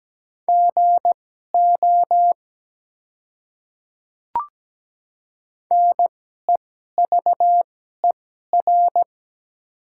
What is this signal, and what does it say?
Morse code, heard as a keyed tone, about 17 wpm, spaced as for 12 wpm, 700 Hz. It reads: GO NEVER
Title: 2x Character Spacing - Top 300 Words - 17wpm